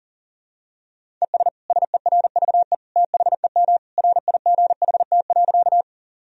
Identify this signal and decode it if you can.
Morse code, heard as a keyed tone, about 37 wpm, 700 Hz. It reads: ES SERVE THEM RIGHT.
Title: Sentences from Top 400 Words and Abbreviations 40wpm